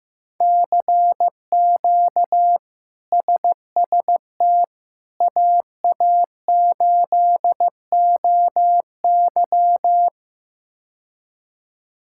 Morse code, heard as a keyed tone, about 16 wpm, 700 Hz. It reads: CQ SST AA8OY
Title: K1USN SST QSO Practice - 15wpm